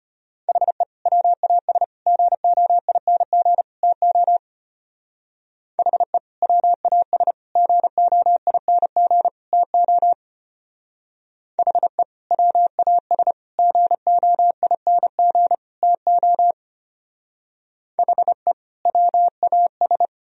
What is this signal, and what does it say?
Morse code, heard as a keyed tone, about 31 wpm, 700 Hz. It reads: HE WAS GOING TO HE WAS GOING TO HE WAS GOING TO HE WAS
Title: Speed-Racing - Sets of 4 Words 25wpm